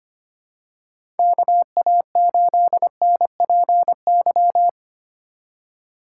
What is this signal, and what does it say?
Morse code, heard as a keyed tone, about 25 wpm, 700 Hz. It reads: KA8NPY
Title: USA Call Signs - 4x Character Spacing - 25wpm